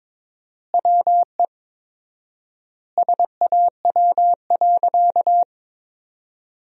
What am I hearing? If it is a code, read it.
Morse code, heard as a keyed tone, about 22 wpm, 700 Hz. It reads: WE SAW.
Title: 4x Word-Spacing - Sentences from Top 300 Words - 22wpm